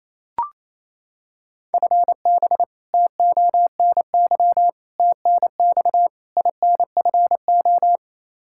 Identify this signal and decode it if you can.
Morse code, heard as a keyed tone, about 27 wpm, 700 Hz. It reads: FB TONY TNX INFO
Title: Practice QSO - Exchange 3 - More Info - 28wpm